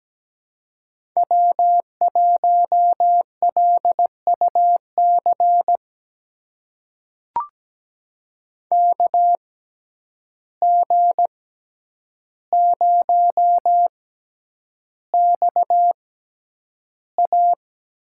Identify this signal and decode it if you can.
Morse code, heard as a keyed tone, about 17 wpm, spaced as for 9 wpm, 700 Hz. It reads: W1LUC KG0XA
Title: USA Call Signs - 5 Characters - 6x Character Spacing - 17wpm